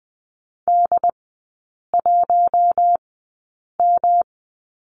Morse code, heard as a keyed tone, about 20 wpm, spaced as for 8 wpm, 700 Hz. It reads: D1M
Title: 107 Lesson 18e - Letter m - Three Characters 20wpm